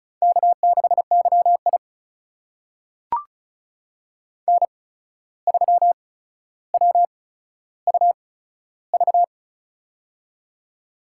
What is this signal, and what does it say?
Morse code, heard as a keyed tone, about 36 wpm, spaced as for 14 wpm, 700 Hz. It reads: K6YI N3WUV
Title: USA Call Signs - 8x Character Spacing - 35wpm